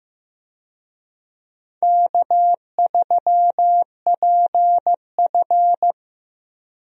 Morse code, heard as a keyed tone, about 16 wpm, 700 Hz. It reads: K3PF
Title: USA Call Signs - 4 Characters - Repeated 2x - 15wpm